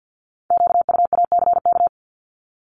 Morse code, heard as a keyed tone, about 50 wpm, 690 Hz. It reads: QUACK
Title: Multi-letter Phonemes 50wpm